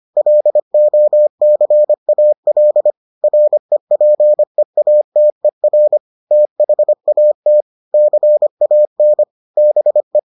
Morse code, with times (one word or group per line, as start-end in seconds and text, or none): 0.2-2.9 LOCAL
3.2-6.0 REPEATER
6.3-7.6 THAT
7.9-9.2 CAN
9.6-10.2 BE